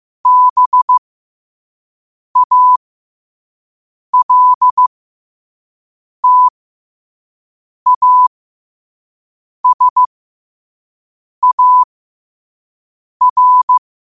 Morse code, heard as a keyed tone, about 15 wpm, spaced as for 5 wpm, 1000 Hz. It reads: BALTASAR
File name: qotdfiles / 5wpm